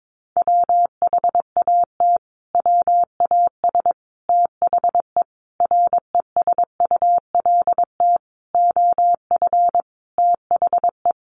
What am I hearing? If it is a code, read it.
Morse code, heard as a keyed tone, about 22 wpm, 700 Hz. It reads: WHAT WAS THE RESULT OF THE
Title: Sentences from Top 1500 Words 22wpm